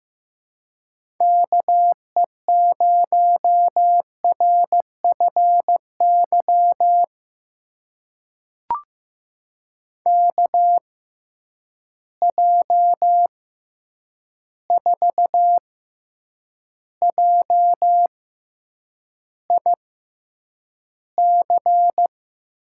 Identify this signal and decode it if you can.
Morse code, heard as a keyed tone, about 15 wpm, spaced as for 8 wpm, 700 Hz. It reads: KE0RFY KJ4JIC
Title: USA Call Signs - 6 Characters - 6x Character Spacing - 15wpm